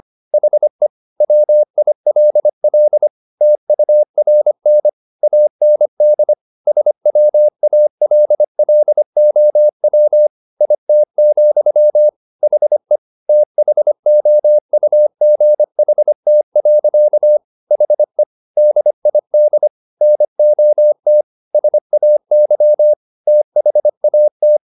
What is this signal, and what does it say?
Morse code, heard as a keyed tone, about 25 wpm, 600 Hz. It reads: HE WILL TURN AND SWALLOW IT, HE THOUGHT. HE DID NOT SAY THAT
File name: the old man and the sea part 1 25wpm